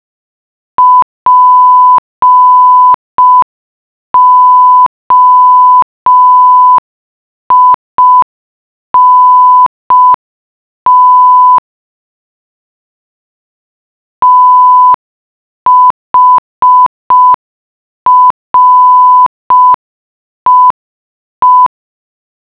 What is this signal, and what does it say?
Morse code, heard as a keyed tone, about 5 wpm, 1000 Hz. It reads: POINT THREE